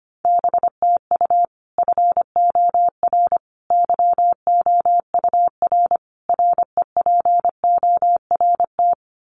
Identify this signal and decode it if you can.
Morse code, heard as a keyed tone, about 25 wpm, 700 Hz. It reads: BTU FOR YOUR REPORT
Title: Common QSO Phrases 25wpm